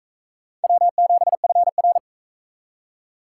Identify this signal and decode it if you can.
Morse code, heard as a keyed tone, about 42 wpm, 700 Hz. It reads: W7FR